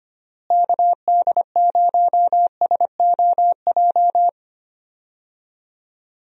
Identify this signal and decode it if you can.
Morse code, heard as a keyed tone, about 25 wpm, 700 Hz. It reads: KD0SOJ
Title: USA Call Signs - 6 Characters - Repeated 2x - 25wpm